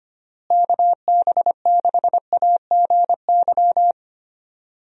Morse code, heard as a keyed tone, about 25 wpm, 700 Hz. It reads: KB6AGY
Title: USA Call Signs - 6 Characters - Repeated 3x - 25wpm